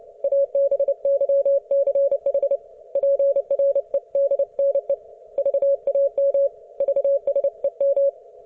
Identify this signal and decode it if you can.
Morse code, heard as a keyed tone, about 31 wpm, 560 Hz. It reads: ABYCH PREDNE VAM VSEM